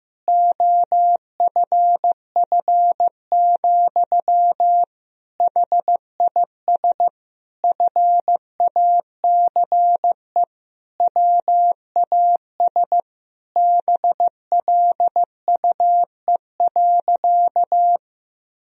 Morse code, heard as a keyed tone, about 16 wpm, 700 Hz. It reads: OFF, HIS FACE WAS BLUE.
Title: Sentences from Top 1000 Words - Review Part 3 - 15wpm